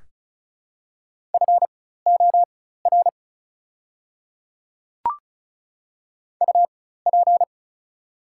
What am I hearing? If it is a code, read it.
Morse code, heard as a keyed tone, about 34 wpm, spaced as for 16 wpm, 700 Hz. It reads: FOR UP
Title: 4x Character Spacing - Top 100 Words - 35wpm